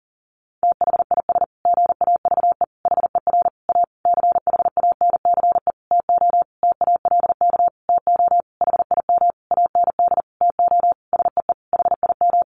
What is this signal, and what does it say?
Morse code, heard as a keyed tone, about 40 wpm, 700 Hz. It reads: THIS GAVE HER A CHANCE TO TALK TO HIM AND TO SEE HIM
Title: Sentences from Top 1400 Words 40wpm